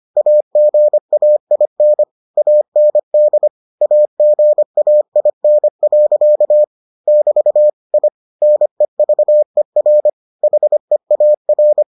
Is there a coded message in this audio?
Yes: Morse code from 0.2 to 11.8 s, heard as a keyed tone, about 25 wpm, 600 Hz. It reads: AGAIN AND AGAIN. <BT> I NEVER HEAR